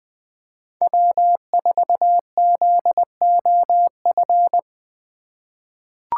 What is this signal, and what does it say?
Morse code, heard as a keyed tone, about 20 wpm, 700 Hz. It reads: W4ZOF
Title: Code Groups - 5 Letter:Numbers - 20wpm